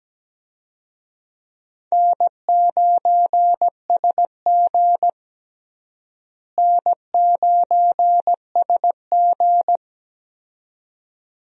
Morse code, heard as a keyed tone, about 17 wpm, 700 Hz. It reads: N9SG N9SG
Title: USA Call Signs - 4 Characters - Repeated 2x - 17wpm